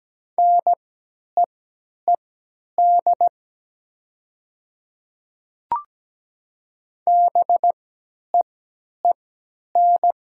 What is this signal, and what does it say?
Morse code, heard as a keyed tone, about 17 wpm, spaced as for 10 wpm, 700 Hz. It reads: NEED BEEN
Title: 3x Character Spacing - Top 200 Words - 17wpm